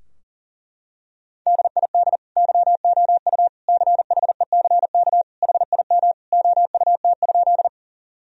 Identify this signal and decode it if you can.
Morse code, heard as a keyed tone, about 37 wpm, 700 Hz. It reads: DID YOU CHECK HIM OUT?
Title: Sentences from Top 1000 Words - Review Part 3 - 40wpm